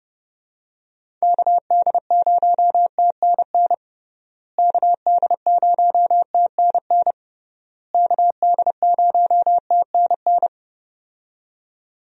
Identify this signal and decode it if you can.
Morse code, heard as a keyed tone, about 30 wpm, 700 Hz. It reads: KD0TNN KD0TNN KD0TNN
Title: USA Call Signs - 6 Characters - Repeated 3x - 30wpm